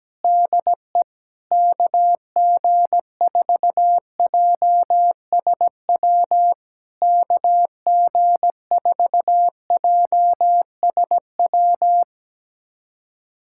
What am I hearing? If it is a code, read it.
Morse code, heard as a keyed tone, about 17 wpm, 700 Hz. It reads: DE KG4JSW KG4JSW
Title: SOTA QSO Practice 17wpm